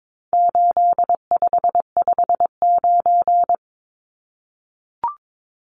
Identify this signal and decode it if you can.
Morse code, heard as a keyed tone, about 22 wpm, 700 Hz. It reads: 8559
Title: Numbers - 4-Digit 22wpm